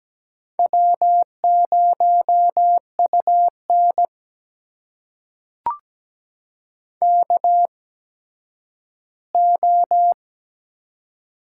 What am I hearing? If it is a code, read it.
Morse code, heard as a keyed tone, about 17 wpm, spaced as for 10 wpm, 700 Hz. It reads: W0UN KO
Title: USA Call Signs - 4 Characters - 8x Character Spacing - 17wpm